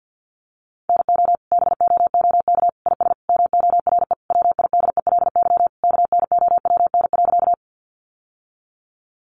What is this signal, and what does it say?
Morse code, heard as a keyed tone, about 50 wpm, 700 Hz. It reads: NO BOOK IS MORE WIDELY KNOWN.
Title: Sentences from Top 1200 Words 50wpm